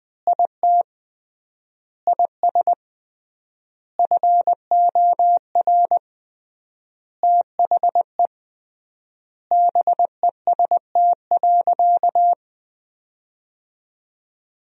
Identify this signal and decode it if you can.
Morse code, heard as a keyed tone, about 21 wpm, 700 Hz. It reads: IT IS FOR THE BEST.
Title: 3x Word-Spacing - Sentences from Top 500 Words - 20wpm